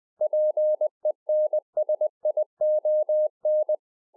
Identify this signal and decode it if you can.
Morse code, heard as a keyed tone, about 20 wpm, 610 Hz. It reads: PENSION
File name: 20WPM